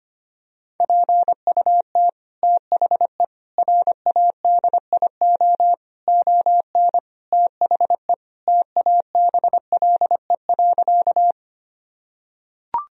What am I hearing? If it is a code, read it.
Morse code, heard as a keyed tone, about 25 wpm, 700 Hz. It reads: PUT THE RADIO ON THE TABLE.
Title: Sentences from Top 1000 Words - Review Part 3 - 25wpm